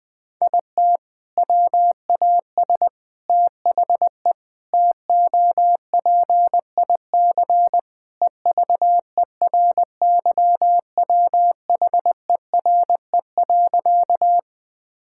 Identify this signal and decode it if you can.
Morse code, heard as a keyed tone, about 20 wpm, 700 Hz. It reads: IT WAS THE TOPIC EVERYWHERE.
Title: Sentences from Top 1600 Words 20wpm